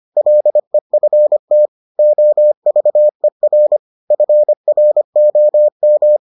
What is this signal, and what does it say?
Morse code, heard as a keyed tone, about 25 wpm, 600 Hz. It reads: LEFT OVER FROM